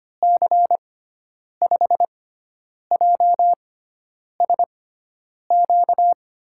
Morse code, heard as a keyed tone, about 25 wpm, spaced as for 9 wpm, 700 Hz. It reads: C5JSQ
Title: USA Call Signs - 6 Characters - 6x Character Spacing - 25wpm